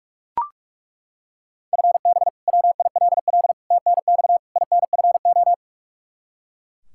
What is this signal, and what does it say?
Morse code, heard as a keyed tone, about 41 wpm, 700 Hz. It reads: FB WILL TNX INFO
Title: Practice QSO - Exchange 3 - More Info - 45wpm